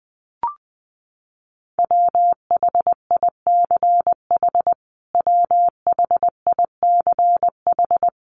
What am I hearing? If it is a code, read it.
Morse code, heard as a keyed tone, about 20 wpm, 700 Hz. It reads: WHICH WHICH
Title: Top 200 Words Repeated 20wpm